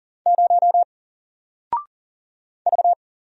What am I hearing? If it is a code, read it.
Morse code, heard as a keyed tone, about 40 wpm, 700 Hz. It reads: 0 V
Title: Single Letter-Number Alt 40wpm